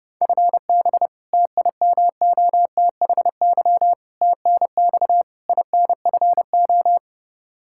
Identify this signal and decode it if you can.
Morse code, heard as a keyed tone, about 29 wpm, 700 Hz. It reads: FB TIMOTHY TNX INFO
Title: Practice QSO - Exchange 3 - More Info - 30wpm